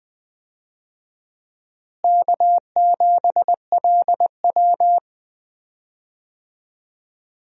Morse code, heard as a keyed tone, about 20 wpm, 700 Hz. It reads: K7LW
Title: USA Call Signs - 4 Characters - Repeated 3x - 20wpm